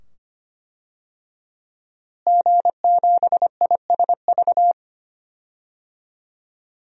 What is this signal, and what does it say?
Morse code, heard as a keyed tone, about 25 wpm, 700 Hz. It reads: G7ISV
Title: DXCC-100 Call Signs - 5 Characters - Repeated 2x - 25wpm